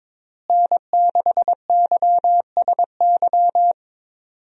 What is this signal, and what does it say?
Morse code, heard as a keyed tone, about 22 wpm, 700 Hz. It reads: N6YSY
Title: USA Call Signs - 5 Characters - Repeated 2x - 22wpm